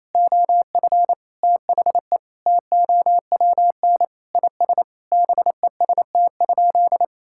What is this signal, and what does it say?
Morse code, heard as a keyed tone, about 28 wpm, 700 Hz. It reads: OF THE TOWN IS BEST?
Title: Sentences from Top 1000 Words - Review Part 5 - 28wpm